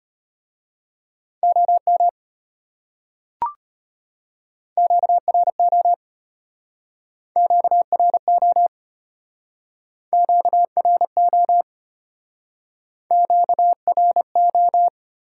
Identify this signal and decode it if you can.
Morse code, heard as a keyed tone, about 32 wpm, 700 Hz. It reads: OM QRO QRO QRO QRO